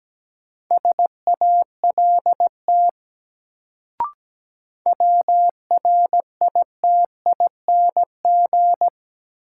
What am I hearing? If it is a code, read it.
Morse code, heard as a keyed tone, about 17 wpm, 700 Hz. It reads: SALT WRITING
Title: Sentences from Top 800 Words - New Word Drill - 17wpm